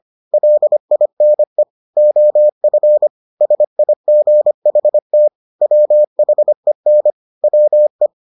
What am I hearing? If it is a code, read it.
Morse code, heard as a keyed tone, about 25 wpm, 600 Hz. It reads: LINE OF SIGHT WHEN WE